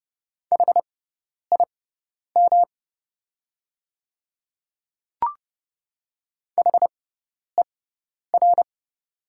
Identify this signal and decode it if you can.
Morse code, heard as a keyed tone, about 30 wpm, spaced as for 10 wpm, 700 Hz. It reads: HIM HER
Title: 6x Character Spacing - Top 200 Words - 30wpm